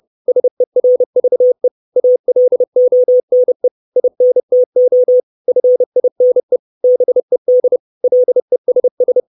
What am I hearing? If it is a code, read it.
Morse code, heard as a keyed tone, about 30 wpm, 500 Hz. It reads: SERVE ALONE INTO FINE BED LESS
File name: 632 Words 30WPM